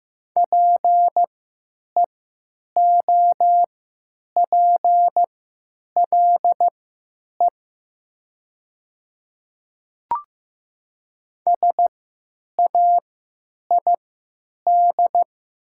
Morse code, heard as a keyed tone, about 15 wpm, spaced as for 9 wpm, 700 Hz. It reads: PEOPLE SAID
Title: 3x Character Spacing - Top 100 Words - 15wpm